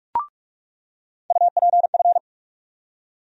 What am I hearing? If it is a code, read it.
Morse code, heard as a keyed tone, about 42 wpm, 700 Hz. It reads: UPF
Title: USA Call Sign Postfixes - 3 Letters 45wpm